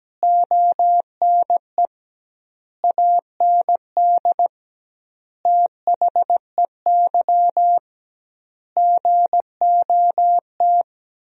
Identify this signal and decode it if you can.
Morse code, heard as a keyed tone, about 17 wpm, 700 Hz. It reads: ONE AND THEY GOT
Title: 2x Word-Spacing - Sentences from Top 300 Words - 17wpm